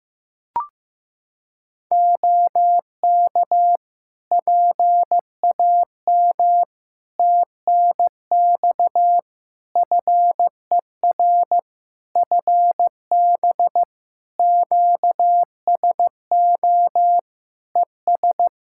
Morse code, heard as a keyed tone, about 15 wpm, 700 Hz. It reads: OK PAM TNX FER FB QSO ES
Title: K1USN SST QSO Practice 15wpm